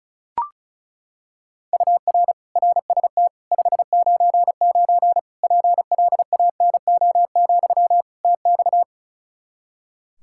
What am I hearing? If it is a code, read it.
Morse code, heard as a keyed tone, about 34 wpm, 700 Hz. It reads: UR RST 599 PLANO, TX
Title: K1USN SST QSO Practice 35wpm